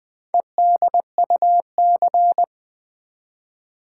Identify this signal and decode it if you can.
Morse code, heard as a keyed tone, about 20 wpm, 700 Hz. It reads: EDUC
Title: Build A Word - 2 Letters At A Time - 8-Letter Words - 20wpm